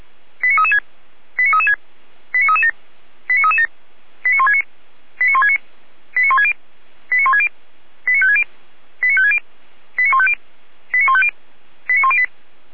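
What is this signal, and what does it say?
Five-tone selective call (ZVEI), heard as quick runs of five tones, tones of about 70 ms. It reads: 89336 89336 89337 89337 89169 89169 89160 89160 89560 89560 89150 89150 89108
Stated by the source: ZVEI five-tone selective call received on an SDR.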